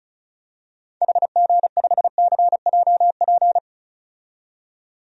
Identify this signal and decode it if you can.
Morse code, heard as a keyed tone, about 34 wpm, 700 Hz. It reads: HG5CJP
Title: DXCC-100 Call Signs - 6 Characters - 35wpm